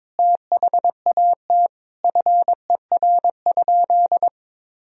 Morse code, heard as a keyed tone, about 21 wpm, 700 Hz. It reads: THAT FER?
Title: Sentences from Top 200 Words and Abbreviations 22wpm